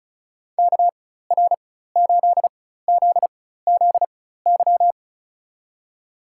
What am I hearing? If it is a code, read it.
Morse code, heard as a keyed tone, about 36 wpm, spaced as for 16 wpm, 700 Hz. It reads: KR8ZZY
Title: USA Call Signs - 4x Character Spacing - 35wpm